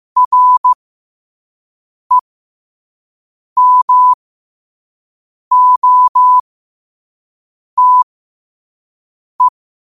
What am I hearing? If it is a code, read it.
Morse code, heard as a keyed tone, about 15 wpm, spaced as for 5 wpm, 1000 Hz. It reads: REMOTE
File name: qotdfiles / 5wpm